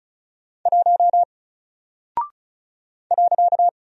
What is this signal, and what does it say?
Morse code, heard as a keyed tone, about 36 wpm, 700 Hz. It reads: 1 .
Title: Single Character (Letter:Number:Punctuation:Prosign) 35wpm